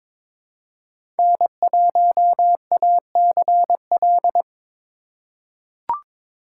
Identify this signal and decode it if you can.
Morse code, heard as a keyed tone, about 22 wpm, 700 Hz. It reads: N1ACL
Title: USA Call Signs - 5 Characters - 4x Character Spacing - 22wpm